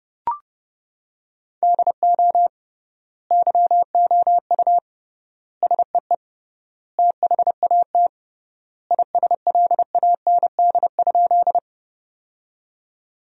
Morse code, heard as a keyed tone, about 30 wpm, 700 Hz. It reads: DO YOU SEE THAT ISLAND?
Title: 3x Word-Spacing - Sentences from Top 500 Words - 30wpm